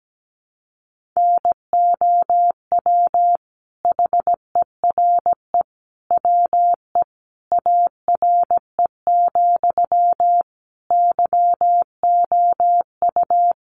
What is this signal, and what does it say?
Morse code, heard as a keyed tone, about 17 wpm, 700 Hz. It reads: NOW HERE WE ARE, YOU
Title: Sentences from Top 1200 Words 17wpm